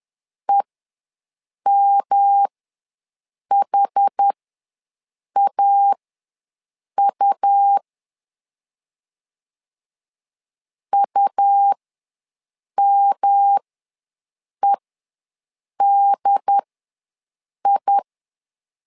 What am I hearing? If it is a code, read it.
Morse code, heard as a keyed tone, about 10 wpm, spaced as for 6 wpm, 800 Hz. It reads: EMHAU UMEDI
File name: LEZIONI / 3 MWRO / 3 M ws 10wpm 2121